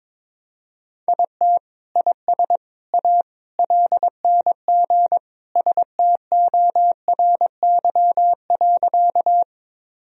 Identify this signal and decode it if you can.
Morse code, heard as a keyed tone, about 22 wpm, 700 Hz. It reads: IT IS A LNG STORY.
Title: Sentences from Top 300 Words and Abbreviations 22wpm